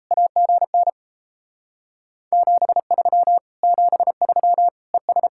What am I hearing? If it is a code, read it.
Morse code, heard as a keyed tone, about 37 wpm, 700 Hz. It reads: AGN 73 73 ES